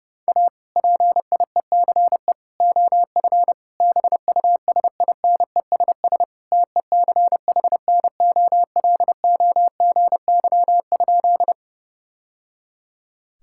Morse code, heard as a keyed tone, about 30 wpm, 700 Hz. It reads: A PIECE OF BUSINESS TECHNOLOGY?
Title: Sentences from Top 1000 Words - Review Part 5 - 30wpm